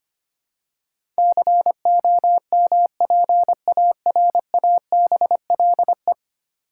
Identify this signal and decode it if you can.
Morse code, heard as a keyed tone, about 25 wpm, 700 Hz. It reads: COMPARABLE
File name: practice / 25wpm / Ten Letter Words - 25wpm